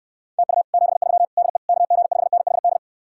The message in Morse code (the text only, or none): TU 73 DE KG5NHD